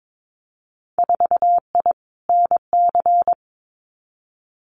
Morse code, heard as a keyed tone, about 22 wpm, 700 Hz. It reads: 4I NC
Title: Winter Field Day Exchanges 22wpm